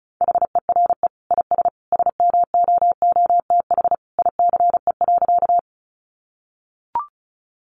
Morse code, heard as a keyed tone, about 34 wpm, 700 Hz. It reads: HERE IS SMOOTH ICE.
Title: Sentences from Top 1400 Words 35wpm